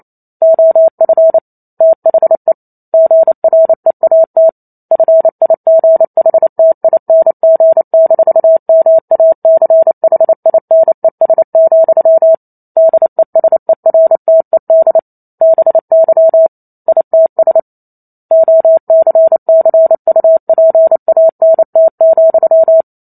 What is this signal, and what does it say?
Morse code, heard as a keyed tone, about 28 wpm, 650 Hz. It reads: OF THE GREAT FIGHTING-MACHINES, DESERTED BY ITS OCCUPANT,